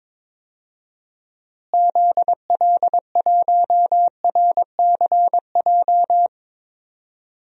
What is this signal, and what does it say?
Morse code, heard as a keyed tone, about 21 wpm, 700 Hz. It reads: ZL1RCJ
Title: DXCC-100 Call Signs - 6 Characters - Repeated 2x - 22wpm